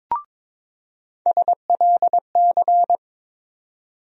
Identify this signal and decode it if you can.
Morse code, heard as a keyed tone, about 22 wpm, 700 Hz. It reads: SLC
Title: USA Call Sign Postfixes - 3 Letters 22wpm